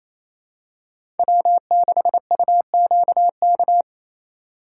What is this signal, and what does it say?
Morse code, heard as a keyed tone, about 28 wpm, 700 Hz. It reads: W6UQK
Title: USA Call Signs - 4x Character Spacing - 28wpm